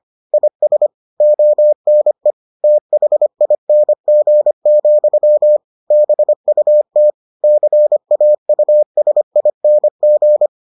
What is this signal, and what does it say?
Morse code, heard as a keyed tone, about 25 wpm, 600 Hz. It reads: IS ONE THING, BUT CAUSING